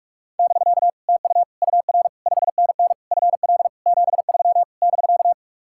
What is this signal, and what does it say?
Morse code, heard as a keyed tone, about 58 wpm, spaced as for 51 wpm, 700 Hz. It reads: <BK> TU UR 5NN FL 73 <BK>
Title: Speed-Racing Common QSO Phrases Encore - 45wpm